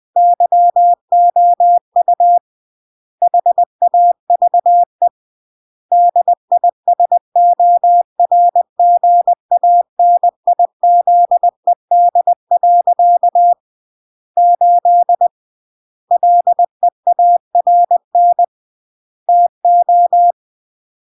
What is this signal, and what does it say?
Morse code, heard as a keyed tone, about 20 wpm, 700 Hz. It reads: YOU HAVE DISORGANIZED. 8 LEARN TO